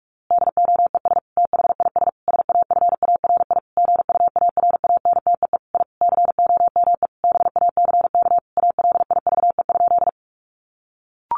Morse code, heard as a keyed tone, about 44 wpm, 700 Hz. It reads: DOES THIS SAFARI GUARANTEE I COME BACK ALIVE?
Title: A Sound Of Thunder 45wpm